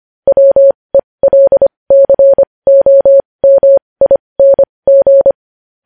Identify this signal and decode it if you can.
Morse code, heard as a keyed tone, about 25 wpm, spaced as for 20 wpm, 550 Hz. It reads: WELCOMING
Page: news headlines in morse code at 20 wpm